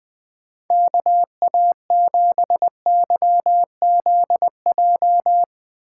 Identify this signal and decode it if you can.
Morse code, heard as a keyed tone, about 20 wpm, 700 Hz. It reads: KA7YZJ
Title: USA Callsign DE Callsign - 6 Characters - 20wpm